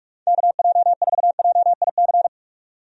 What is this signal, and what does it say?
Morse code, heard as a keyed tone, about 44 wpm, 700 Hz. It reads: KJ4JIC
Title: USA Call Signs - 6 Characters - 45wpm